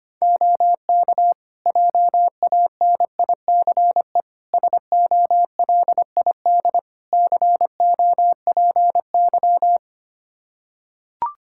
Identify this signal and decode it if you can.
Morse code, heard as a keyed tone, about 25 wpm, 700 Hz. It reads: OK JANICE SOLID COPY
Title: Practice QSO - Exchange 3 - More Info - 25wpm